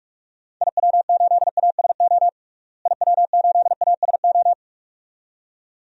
Morse code, heard as a keyed tone, about 42 wpm, 690 Hz. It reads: IW8ASO IW8ASO
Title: DXCC-100 Call Signs - 6 Characters - Repeated 2x - 45wpm